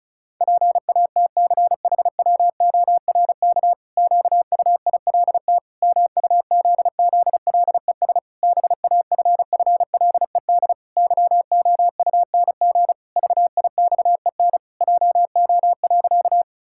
Morse code, heard as a keyed tone, about 36 wpm, 700 Hz. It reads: PATCHWORK QUILT MUZZLES BAFFLED YOUNG VIXEN JO.
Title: Pangram Challenge v2 - 35wpm